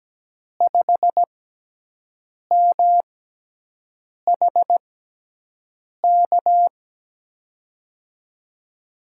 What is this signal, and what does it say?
Morse code, heard as a keyed tone, about 17 wpm, spaced as for 6 wpm, 700 Hz. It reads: 5MHK
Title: USA Call Signs - 6 Characters - 6x Character Spacing - 17wpm